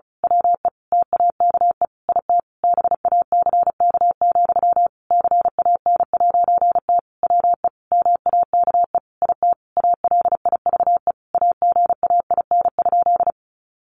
Morse code, heard as a keyed tone, about 34 wpm, 700 Hz. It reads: WE TAKE IT BACK, CAN'T WE MAKE IT ALIVE AGAIN?
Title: A Sound Of Thunder 35wpm